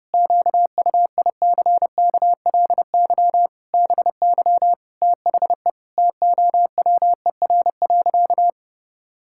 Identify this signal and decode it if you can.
Morse code, heard as a keyed tone, about 30 wpm, 700 Hz. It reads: QUICKLY BY THE TOWER.
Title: Pangram Challenge v2 - 30wpm